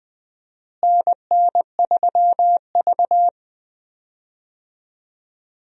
Morse code, heard as a keyed tone, about 21 wpm, 700 Hz. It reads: NN3V
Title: USA Call Signs - 4 Characters - Repeated 2x - 20wpm